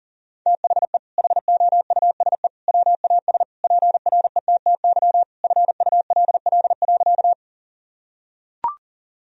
Morse code, heard as a keyed tone, about 40 wpm, 700 Hz. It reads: THE HOUSE WAS PRETTY FULL.
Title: Sentences from Top 1000 Words - Review Part 3 - 40wpm